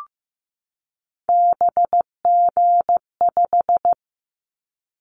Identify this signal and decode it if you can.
Morse code, heard as a keyed tone, about 15 wpm, 700 Hz. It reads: BG5
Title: Callsign Prefixes for Top 100 DXCC - 15wpm